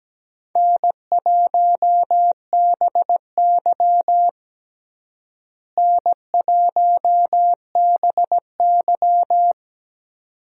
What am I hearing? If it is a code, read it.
Morse code, heard as a keyed tone, about 17 wpm, 700 Hz. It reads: N1BY N1BY
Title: USA Call Signs - 4 Characters - Repeated 2x - 17wpm